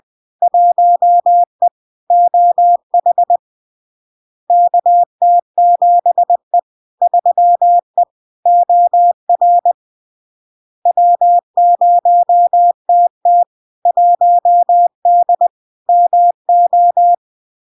Morse code, heard as a keyed tone, about 20 wpm, 700 Hz. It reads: 1E OH KT7E 3E OR W0TT 1D MO
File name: LARK-FD-2022-Log-in-CW-20wpm-700-Hz